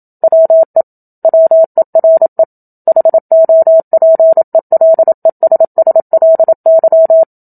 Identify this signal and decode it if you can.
Morse code, heard as a keyed tone, about 27 wpm, 650 Hz. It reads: WE WERE HOPELESSLY
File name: Book Two / 01-Chapter 1 - 27 WPM